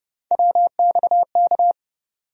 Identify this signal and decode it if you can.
Morse code, heard as a keyed tone, about 30 wpm, 700 Hz. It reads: WXK
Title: USA Call Sign Postfixes - 3 Letters 30wpm